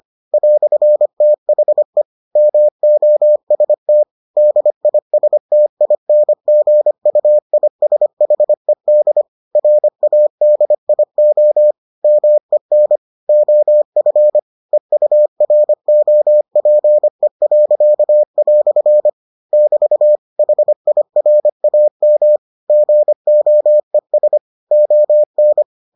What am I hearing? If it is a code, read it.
Morse code, heard as a keyed tone, about 25 wpm, 600 Hz. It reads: "THE MOST DISTINGUISHED RADIO MEN OF EUROPE." <BT> HIRAM GOES ON